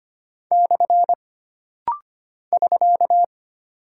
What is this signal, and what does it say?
Morse code, heard as a keyed tone, about 25 wpm, 700 Hz. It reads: / <SK>
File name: Single Punctuation-Prosign - Rapid-Fire - 25wpm